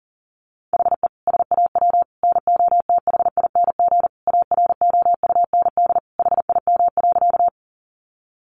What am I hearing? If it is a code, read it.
Morse code, heard as a keyed tone, about 40 wpm, 700 Hz. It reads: HE SAW NOTHING AROUND HIM.
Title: Sentences from Top 600 Words 40wpm